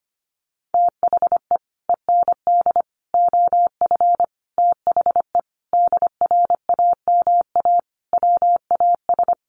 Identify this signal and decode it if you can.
Morse code, heard as a keyed tone, about 25 wpm, 700 Hz. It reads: THE END OF THE DRAMA WAS
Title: Sentences from Top 1500 Words 25wpm